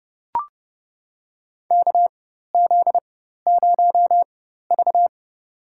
Morse code, heard as a keyed tone, about 32 wpm, spaced as for 14 wpm, 700 Hz. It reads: KZ0V
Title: USA Call Signs - 4 Characters - 4x Character Spacing - 30wpm